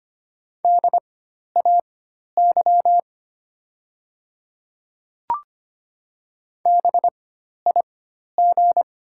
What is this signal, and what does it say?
Morse code, heard as a keyed tone, about 25 wpm, spaced as for 12 wpm, 700 Hz. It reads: DAY BIG
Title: 4x Character Spacing - Top 200 Words - 25wpm